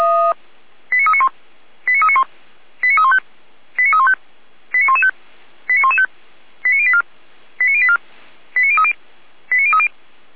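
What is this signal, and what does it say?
Five-tone selective call (ZVEI), heard as quick runs of five tones, tones of about 70 ms. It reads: 89391 89391 89315 89315 89115 89115 89084 89084 89030 89030
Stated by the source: ZVEI five-tone selective call received on an SDR.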